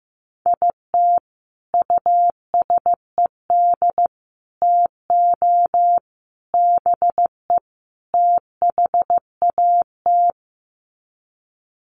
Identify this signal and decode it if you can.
Morse code, heard as a keyed tone, about 15 wpm, 700 Hz. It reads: IT USED TO BE THAT
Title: Sets of 5 Words Encore2 15wpm